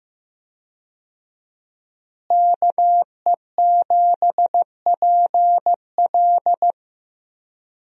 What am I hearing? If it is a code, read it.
Morse code, heard as a keyed tone, about 15 wpm, 700 Hz. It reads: KE7PL
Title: USA Call Signs - 5 Characters - Repeated 2x - 15wpm